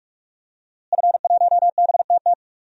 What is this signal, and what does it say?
Morse code, heard as a keyed tone, about 43 wpm, 700 Hz. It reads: F1BTT
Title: DXCC-100 Call Signs - 5 Characters - Repeated 3x - 45wpm